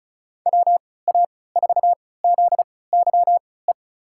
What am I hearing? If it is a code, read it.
Morse code, heard as a keyed tone, about 36 wpm, spaced as for 20 wpm, 700 Hz. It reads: WA4ZYE
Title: USA Call Signs - 3x Character Spacing - 35wpm